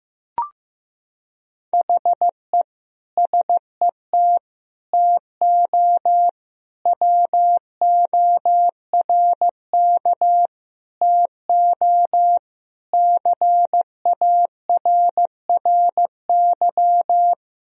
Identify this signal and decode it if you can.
Morse code, heard as a keyed tone, about 15 wpm, 700 Hz. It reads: HE SET TO WORK TO CARRY
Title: Sentences from Top 1000 Words - Review Part 1 - 15wpm